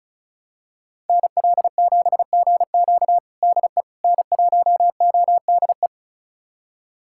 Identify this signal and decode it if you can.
Morse code, heard as a keyed tone, about 36 wpm, 700 Hz. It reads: NL7GQ DE N1ODE
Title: USA Callsign DE Callsign - 5 Characters - 35wpm